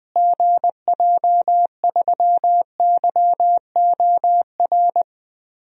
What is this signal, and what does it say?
Morse code, heard as a keyed tone, about 20 wpm, 700 Hz. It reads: GJ3YOR
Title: DXCC-100 Call Signs - 6 Characters - Repeated 3x - 20wpm